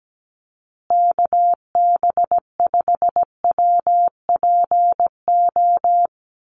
Morse code, heard as a keyed tone, about 17 wpm, 700 Hz. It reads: KB5WPO
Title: USA Call Signs Repeated 2x 17wpm